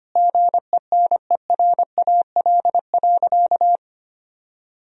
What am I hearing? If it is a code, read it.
Morse code, heard as a keyed tone, about 25 wpm, 700 Hz. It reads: GENERAL.
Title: Sentences from Top 1000 Words - Review Part 3 - 25wpm